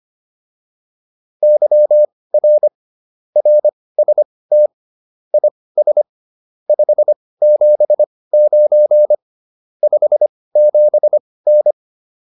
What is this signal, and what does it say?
Morse code, heard as a keyed tone, about 25 wpm, spaced as for 18 wpm, 600 Hz. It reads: YR RST IS 579 57N
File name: QSO304 18